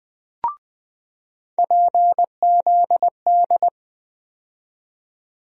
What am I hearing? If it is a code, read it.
Morse code, heard as a keyed tone, about 20 wpm, 700 Hz. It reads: PZD
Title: USA Call Sign Postfixes - 3 Letters 20wpm